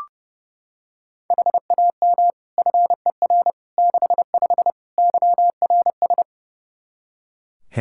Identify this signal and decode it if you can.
Morse code, heard as a keyed tone, about 32 wpm, 700 Hz. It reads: HAM FER 65 YRS
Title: Practice QSO - Exchange 3 - More Info - 30wpm